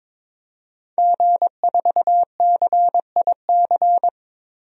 Morse code, heard as a keyed tone, about 22 wpm, 700 Hz. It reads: G4CIC
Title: DXCC-100 Call Signs - 5 Characters - Repeated 3x - 22wpm